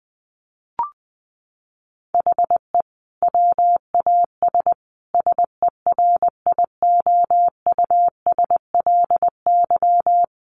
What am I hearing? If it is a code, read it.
Morse code, heard as a keyed tone, about 20 wpm, 700 Hz. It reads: HE WAS SERIOUSLY
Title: Sentences from Top 1200 Words 20wpm